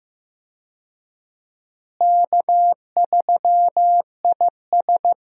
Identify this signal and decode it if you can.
Morse code, heard as a keyed tone, about 15 wpm, 700 Hz. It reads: K3IS
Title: USA Call Signs - 4 Characters - Repeated 2x - 15wpm